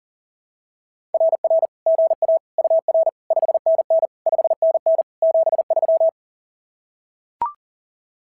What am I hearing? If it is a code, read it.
Morse code, heard as a keyed tone, about 36 wpm, 660 Hz. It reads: RR GA UR 5NN 5NN 73
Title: SOTA QSO Practice 40wpm